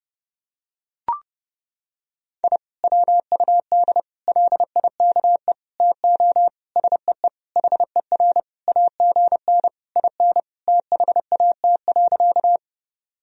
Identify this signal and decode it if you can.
Morse code, heard as a keyed tone, about 30 wpm, 700 Hz. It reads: I WUD LIKE TO SEE HER AGN IN THAT.
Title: Sentences from Top 200 Words and Abbreviations 30wpm